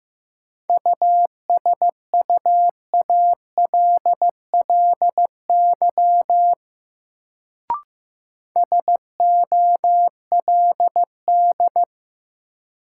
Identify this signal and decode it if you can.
Morse code, heard as a keyed tone, about 15 wpm, 700 Hz. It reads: USUALLY SOLD
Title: Sentences from Top 1400 Words - New Word Drill Alt - 15wpm